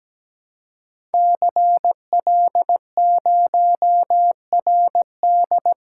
Morse code, heard as a keyed tone, about 17 wpm, 700 Hz. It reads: CL0RD
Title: Code Groups - 5 Letter:Numbers - 17wpm